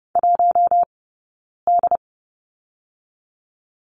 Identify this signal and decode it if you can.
Morse code, heard as a keyed tone, about 30 wpm, 700 Hz. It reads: 1 D
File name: Lesson 19d - Letter w - Two Characters 30wpm